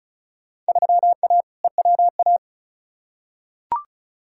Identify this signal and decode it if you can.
Morse code, heard as a keyed tone, about 34 wpm, 700 Hz. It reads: 3A EWA
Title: Field Day Exchanges - 35wpm